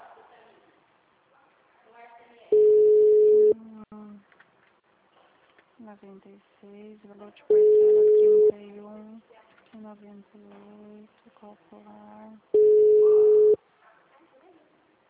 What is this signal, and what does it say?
Ringback tone (Europe, 425 Hz), with speech.